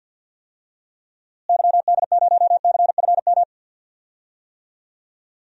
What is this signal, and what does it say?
Morse code, heard as a keyed tone, about 51 wpm, 700 Hz. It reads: YD0CFM
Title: DXCC-100 Call Signs - 6 Characters - Repeated 3x - 50wpm